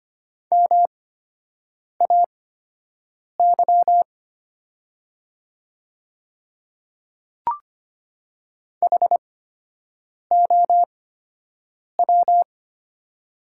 Morse code, heard as a keyed tone, about 25 wpm, spaced as for 7 wpm, 700 Hz. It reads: MAY HOW
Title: 8x Character Spacing - Top 200 Words - 25wpm